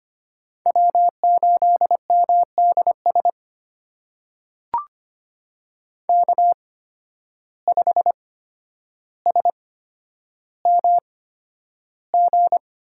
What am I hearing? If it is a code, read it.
Morse code, heard as a keyed tone, about 25 wpm, spaced as for 11 wpm, 700 Hz. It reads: W8MDS K5SMG
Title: USA Call Signs - 5 Characters - 8x Character Spacing - 25wpm